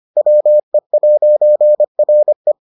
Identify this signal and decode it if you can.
Morse code, heard as a keyed tone, about 25 wpm, 600 Hz. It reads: WE'RE